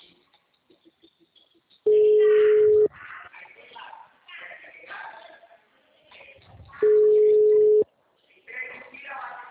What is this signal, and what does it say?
Ringback tone (Europe, 425 Hz), with speech.
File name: out- -2011- - - .34847